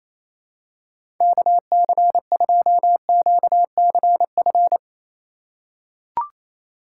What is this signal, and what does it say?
Morse code, heard as a keyed tone, about 28 wpm, 700 Hz. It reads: KC2QCF
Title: USA Call Signs - 6 Characters - 4x Character Spacing - 28wpm